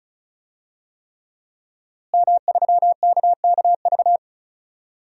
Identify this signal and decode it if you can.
Morse code, heard as a keyed tone, about 36 wpm, 700 Hz. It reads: M3KKV
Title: DXCC-100 Call Signs - 5 Characters - Repeated 3x - 35wpm